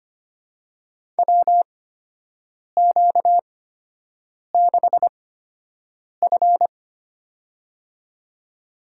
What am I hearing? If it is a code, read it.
Morse code, heard as a keyed tone, about 25 wpm, spaced as for 7 wpm, 700 Hz. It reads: WQ6F
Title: USA Call Signs - 8x Character Spacing - 25wpm